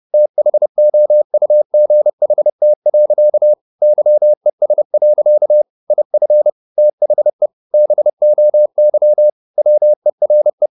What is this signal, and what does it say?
Morse code, heard as a keyed tone, about 30 wpm, 600 Hz. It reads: THOUGHT. YES. IF THE BOY WERE